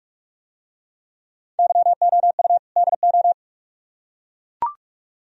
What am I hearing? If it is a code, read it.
Morse code, heard as a keyed tone, about 44 wpm, 700 Hz. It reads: YOU DO